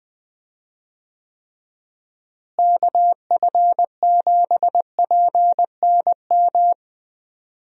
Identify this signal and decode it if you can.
Morse code, heard as a keyed tone, about 20 wpm, 700 Hz. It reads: KF7PNM
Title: USA Call Signs - 6 Characters - Repeated 2x - 20wpm